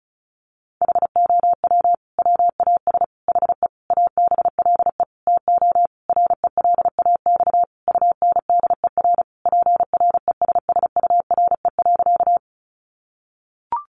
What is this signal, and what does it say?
Morse code, heard as a keyed tone, about 34 wpm, 700 Hz. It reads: HOW WAS HE ABLE TO RELAX UNDER PRESSURE.
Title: Sentences from Top 1400 Words 35wpm